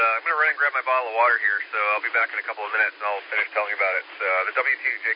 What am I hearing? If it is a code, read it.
Emergency Alert System (SAME) tones, heard as harsh buzzing bursts, with speech. It is heard from 0.3 to 4.1 s.